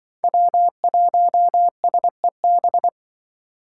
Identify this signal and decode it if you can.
Morse code, heard as a keyed tone, about 25 wpm, 700 Hz. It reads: W1SEB
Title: Speed Racing - USA 1x3 Call Signs 20wpm